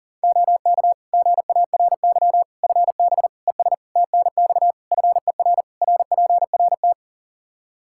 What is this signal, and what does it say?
Morse code, heard as a keyed tone, about 37 wpm, 700 Hz. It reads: OK GARY FB ES TNX FER RPRT
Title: Practice QSO - Exchange 2 - Info - 40wpm